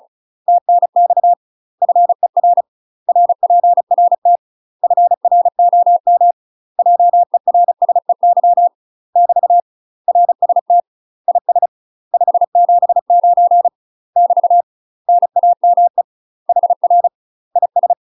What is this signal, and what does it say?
Morse code, heard as a keyed tone, about 36 wpm, 700 Hz. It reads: TNX FER RPRT FROM JERSEY <BT> RST IS 579 <BT> NAME HR IS
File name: UK MASTER QSOs TRAINING / 35WPM / UK MASTER QSOs 0029